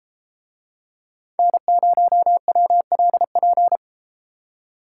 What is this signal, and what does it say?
Morse code, heard as a keyed tone, about 31 wpm, 700 Hz. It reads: N0WLP